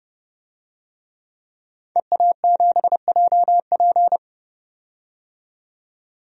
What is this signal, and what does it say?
Morse code, heard as a keyed tone, about 30 wpm, 700 Hz. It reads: EA7JP
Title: DXCC-100 Call Signs - 5 Characters - 30wpm